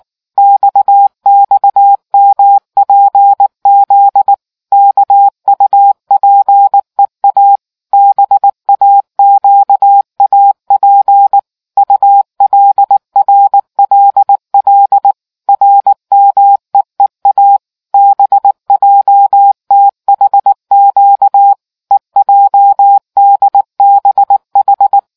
Morse code, heard as a keyed tone, about 19 wpm, 800 Hz. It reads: XXMPZ KUPEA BAQAP ULRLL RMEEA BJTHQ EJDBH